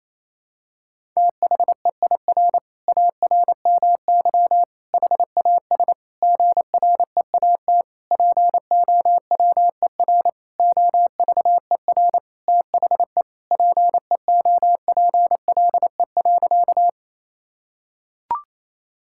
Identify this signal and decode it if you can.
Morse code, heard as a keyed tone, about 28 wpm, 700 Hz. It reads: THEIR ARMY HAS GREAT POWER OVER THE PEOPLE.
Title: Sentences from Top 1000 Words - Review Part 5 - 28wpm